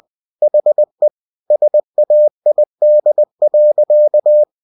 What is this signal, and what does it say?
Morse code, heard as a keyed tone, about 20 wpm, 600 Hz. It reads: HE SAID.
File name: cw-books / winnie-the-pooh / winnie-20wpm-page-0028